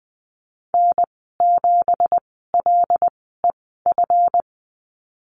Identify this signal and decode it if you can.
Morse code, heard as a keyed tone, about 19 wpm, spaced as for 14 wpm, 700 Hz. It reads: N7LEF
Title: 219 Lesson 35g - Number 0 - Call Signs 20wpm